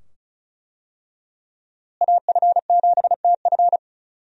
Morse code, heard as a keyed tone, about 36 wpm, 700 Hz. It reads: AF7TF
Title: USA Call Signs - 5 Characters - Repeated 2x - 35wpm